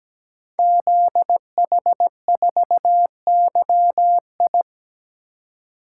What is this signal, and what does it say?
Morse code, heard as a keyed tone, about 17 wpm, 700 Hz. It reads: ZH4YI
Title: Code Groups - 5 Letter:Numbers - 17wpm